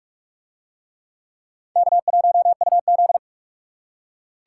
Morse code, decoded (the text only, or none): K1UZ